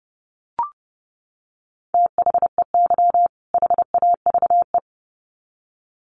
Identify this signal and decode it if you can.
Morse code, heard as a keyed tone, about 30 wpm, 700 Hz. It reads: THEY HAVE
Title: Sets of 2 Words 30wpm